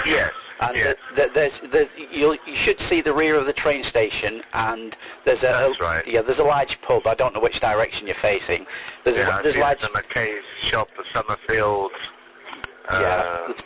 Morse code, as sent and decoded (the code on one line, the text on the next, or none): none